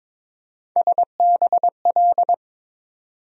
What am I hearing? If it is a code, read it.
Morse code, heard as a keyed tone, about 23 wpm, 700 Hz. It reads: SBL
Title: USA Call Sign Postfixes - 3 Letters 22wpm